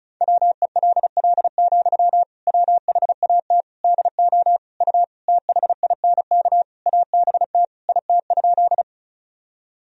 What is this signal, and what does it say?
Morse code, heard as a keyed tone, about 36 wpm, 700 Hz. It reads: WELL, WHAT DO U THINK ABT IT?
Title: Sentences from Top 200 Words and Abbreviations 35wpm